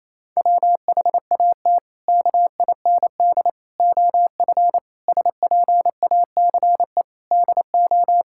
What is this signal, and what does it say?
Morse code, heard as a keyed tone, about 28 wpm, 700 Hz. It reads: WHAT KIND OF SPACE DO
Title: Sentences from Top 1000 Words - Review Part 5 - 28wpm